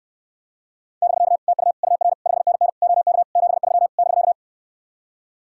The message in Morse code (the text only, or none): <BK> TU UR 5NN OK 73 <BK>